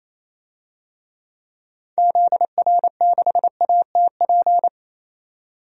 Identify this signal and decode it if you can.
Morse code, heard as a keyed tone, about 28 wpm, 700 Hz. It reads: ZR6ATP
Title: DXCC-100 Call Signs - 6 Characters - 28wpm